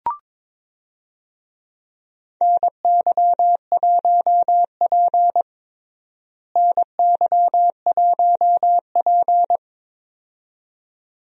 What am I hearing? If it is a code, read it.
Morse code, heard as a keyed tone, about 21 wpm, 700 Hz. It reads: NY1P NY1P
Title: USA Call Signs - 4 Characters - Repeated 2x - 22wpm